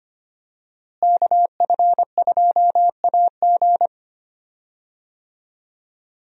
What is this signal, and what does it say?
Morse code, heard as a keyed tone, about 25 wpm, 700 Hz. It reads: KF2AG
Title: USA Call Signs - 5 Characters - Repeated 3x - 25wpm